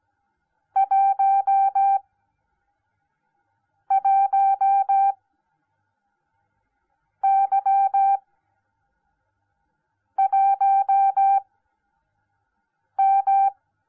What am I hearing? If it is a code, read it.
Morse code, heard as a keyed tone, about 17 wpm, spaced as for 4 wpm, 780 Hz. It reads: 11Y1M